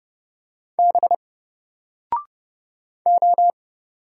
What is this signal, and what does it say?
Morse code, heard as a keyed tone, about 29 wpm, 700 Hz. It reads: B O
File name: Single-Letter-Number-30wpm